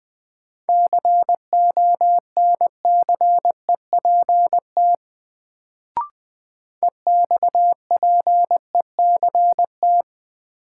Morse code, heard as a keyed tone, about 20 wpm, 700 Hz. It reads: CONCEPT EXPECT
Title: Sentences from Top 1100 Words - New Word Drill Alt - 20wpm